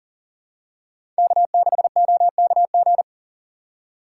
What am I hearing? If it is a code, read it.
Morse code, heard as a keyed tone, about 40 wpm, 700 Hz. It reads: K6OKG
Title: USA Call Signs - 6x Character Spacing - 40wpm